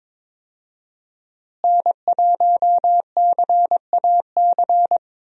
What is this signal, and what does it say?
Morse code, heard as a keyed tone, about 22 wpm, 700 Hz. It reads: N1CAC
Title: USA Call Signs - 5 Characters - Repeated 2x - 22wpm